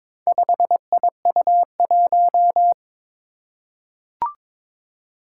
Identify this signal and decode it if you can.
Morse code, heard as a keyed tone, about 22 wpm, 700 Hz. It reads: 5IU1
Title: Four Challenging Letter-Number Combinations 22wpm